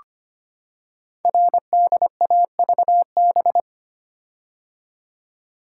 Morse code, heard as a keyed tone, about 25 wpm, 700 Hz. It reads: RDAVB
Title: Code Groups - 5 Letters - 25wpm